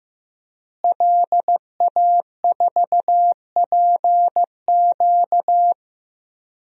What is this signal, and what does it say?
Morse code, heard as a keyed tone, about 15 wpm, 700 Hz. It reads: LA4PQ
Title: DXCC-100 Call Signs - 5 Characters - Repeated 2x - 15wpm